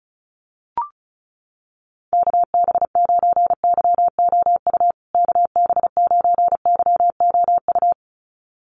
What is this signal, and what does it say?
Morse code, heard as a keyed tone, about 34 wpm, 700 Hz. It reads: KB9YOU KB9YOU
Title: USA Call Signs Repeated 2x 35wpm